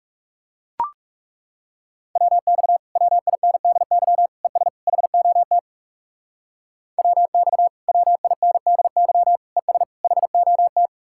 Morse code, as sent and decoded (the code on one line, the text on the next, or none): .-- -..- / .-- .. -. -.. -.-- / . ... / .... --- - / .-- -..- / .-- .. -. -.. -.-- / . ... / .... --- -
WX WINDY ES HOT WX WINDY ES HOT